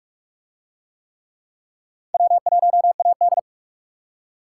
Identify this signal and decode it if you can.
Morse code, heard as a keyed tone, about 44 wpm, 700 Hz. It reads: W1AD